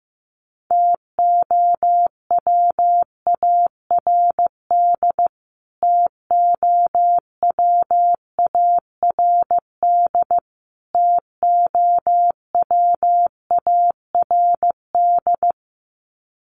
Morse code, heard as a keyed tone, about 15 wpm, 700 Hz. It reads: TOWARD TOWARD TOWARD
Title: Top 500 Words Repeated 15wpm